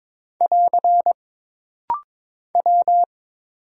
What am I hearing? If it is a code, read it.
Morse code, heard as a keyed tone, about 22 wpm, 700 Hz. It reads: <AR> W
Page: This rapid-fire timing is geared to push your ICR (Instant Character Recognition) skills.